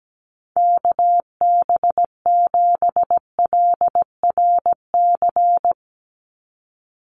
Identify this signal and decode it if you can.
Morse code, heard as a keyed tone, about 17 wpm, 700 Hz. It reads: KB7LRC
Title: USA Call Signs 17wpm